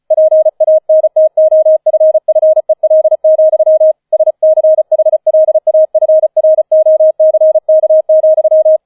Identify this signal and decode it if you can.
Morse code, heard as a keyed tone, about 34 wpm, 630 Hz. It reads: PANTOFFEL, SCHLAFROCK,
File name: 005 Fünfter Streich 35wpm